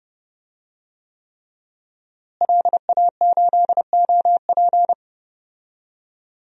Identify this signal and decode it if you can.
Morse code, heard as a keyed tone, about 30 wpm, 700 Hz. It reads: LA8OP
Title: DXCC-100 Call Signs - 5 Characters - Repeated 3x - 30wpm